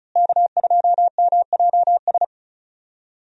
Morse code, heard as a keyed tone, about 36 wpm, 700 Hz. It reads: K2MJS
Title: USA Call Signs - 5 Characters - Repeated 3x - 35wpm